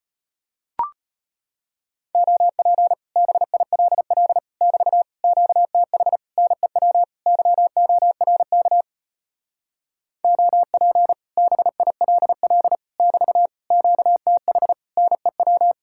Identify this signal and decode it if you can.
Morse code, heard as a keyed tone, about 33 wpm, 700 Hz. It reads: OP BILL <BT> QTH NEW YORK OP BILL <BT> QTH NEW